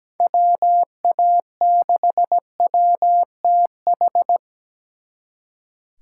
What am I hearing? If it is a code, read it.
Morse code, heard as a keyed tone, about 20 wpm, 700 Hz. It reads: WA6WTH
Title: Speed Racing - USA Call Signs 17wpm